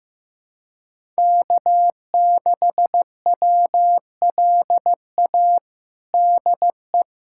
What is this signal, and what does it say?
Morse code, heard as a keyed tone, about 15 wpm, 700 Hz. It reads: K6WLA DE
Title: USA Callsign DE Callsign 15wpm